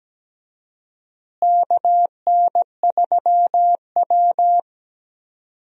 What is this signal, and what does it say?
Morse code, heard as a keyed tone, about 17 wpm, 700 Hz. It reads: KN3W
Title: USA Call Signs - 4 Characters - Repeated 2x - 17wpm